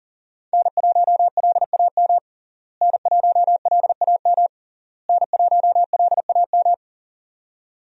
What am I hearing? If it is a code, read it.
Morse code, heard as a keyed tone, about 40 wpm, 700 Hz. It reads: N1LAM N1LAM N1LAM
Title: USA Call Signs - 5 Characters - Repeated 3x - 40wpm